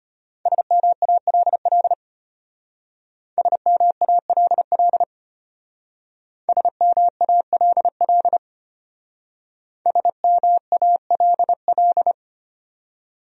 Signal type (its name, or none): Morse code